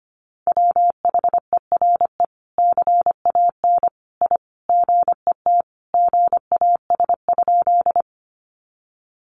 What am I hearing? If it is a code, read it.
Morse code, heard as a keyed tone, about 25 wpm, 700 Hz. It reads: WHERE CAN I GET GAS?
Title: Sentences from Top 1100 Words 25wpm